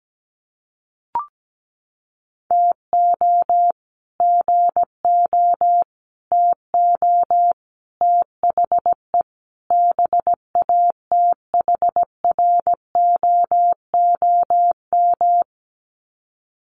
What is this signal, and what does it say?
Morse code, heard as a keyed tone, about 17 wpm, 700 Hz. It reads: TO GO TO THE BATHROOM
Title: Sets of 5 Words Encore 17wpm